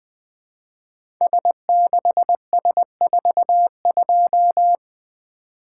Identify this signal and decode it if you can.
Morse code, heard as a keyed tone, about 20 wpm, 700 Hz. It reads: S6S42
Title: Code Groups - 5 Letter:Numbers - 20wpm